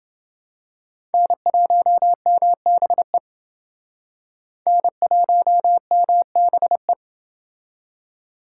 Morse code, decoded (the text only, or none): N1MBE N1MBE